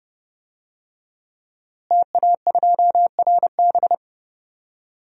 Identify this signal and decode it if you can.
Morse code, heard as a keyed tone, about 30 wpm, 700 Hz. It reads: TA2RB
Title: DXCC-100 Call Signs - 5 Characters - Repeated 2x - 30wpm